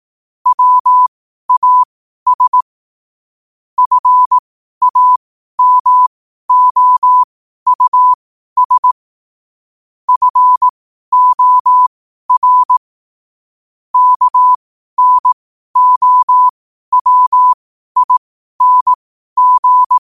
Morse code, heard as a keyed tone, about 18 wpm, spaced as for 12 wpm, 1000 Hz. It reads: WAS FAMOUS FOR KNOWING
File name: qotdfiles / 12wpm